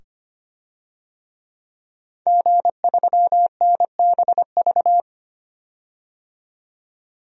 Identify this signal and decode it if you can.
Morse code, heard as a keyed tone, about 25 wpm, 700 Hz. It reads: G3NBV
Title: DXCC-100 Call Signs - 5 Characters - 25wpm